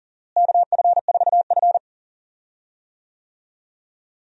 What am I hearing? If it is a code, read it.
Morse code, heard as a keyed tone, about 40 wpm, 700 Hz. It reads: KF4F
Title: USA Call Signs - 4 Characters - Repeated 3x - 40wpm